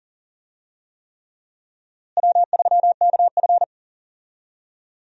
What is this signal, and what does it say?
Morse code, heard as a keyed tone, about 40 wpm, 700 Hz. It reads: W3KF
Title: USA Call Signs - 4 Characters - Repeated 3x - 40wpm